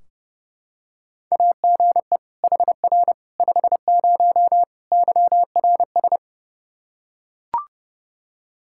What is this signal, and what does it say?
Morse code, heard as a keyed tone, about 32 wpm, 700 Hz. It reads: AGE HR 50 YRS
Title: K1USN SST QSO Practice 30wpm